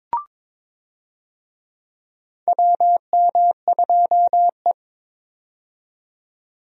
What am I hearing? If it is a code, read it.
Morse code, heard as a keyed tone, about 22 wpm, 700 Hz. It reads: WM2E
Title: USA Call Signs - 4 Characters - 22wpm